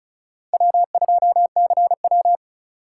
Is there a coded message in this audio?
Yes: Morse code from 0.5 to 2.4 s, heard as a keyed tone, about 45 wpm, 700 Hz. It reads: W2CW